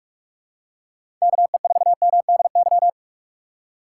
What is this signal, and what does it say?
Morse code, heard as a keyed tone, about 42 wpm, 700 Hz. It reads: KE4MDY
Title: USA Call Signs - 6 Characters - 3x Character Spacing - 45wpm